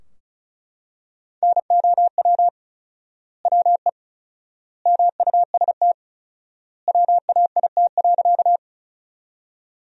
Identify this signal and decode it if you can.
Morse code, heard as a keyed tone, about 36 wpm, 700 Hz. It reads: NOW WE MUST WAIT.
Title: 4x Word-Spacing - Sentences from Top 500 Words - 35wpm